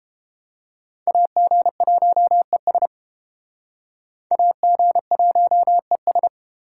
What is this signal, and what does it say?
Morse code, heard as a keyed tone, about 32 wpm, 700 Hz. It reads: AG1ES AG1ES